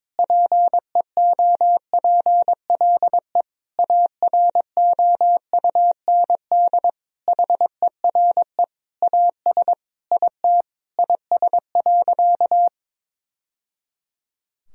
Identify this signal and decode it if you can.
Morse code, heard as a keyed tone, about 22 wpm, 700 Hz. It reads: PEOPLE AROUND HERE AS IT IS.
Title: Sentences from Top 1000 Words - Review Part 3 - 22wpm